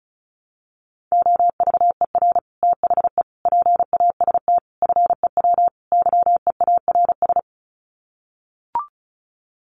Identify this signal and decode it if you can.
Morse code, heard as a keyed tone, about 34 wpm, 700 Hz. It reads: OVER THE PAST FEW YEARS
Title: Sets of 5 Words Encore 35wpm